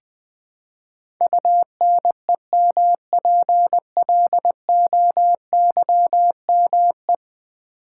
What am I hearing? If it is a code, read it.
Morse code, heard as a keyed tone, about 20 wpm, 700 Hz. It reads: UNEMPLOYME
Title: Build A Word - 5 Letters At A Time - 12-Letter Words - 20wpm